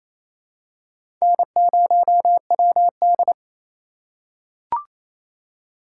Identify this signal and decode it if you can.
Morse code, heard as a keyed tone, about 28 wpm, 700 Hz. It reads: N0WD
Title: USA Call Signs - 4 Characters - 3x Character Spacing - 28wpm